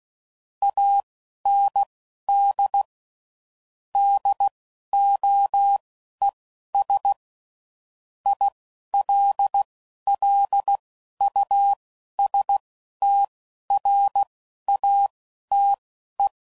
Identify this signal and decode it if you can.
Morse code, heard as a keyed tone, about 16 wpm, spaced as for 11 wpm, 800 Hz. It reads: AND DOES ILLUSTRATE
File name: CW Code Groups @10WPM / Technical Subjects / AND Gate